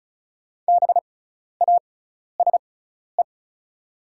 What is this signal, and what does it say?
Morse code, heard as a keyed tone, about 34 wpm, spaced as for 12 wpm, 700 Hz. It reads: BASE
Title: 6x Character Spacing - Top 300 Words - 35wpm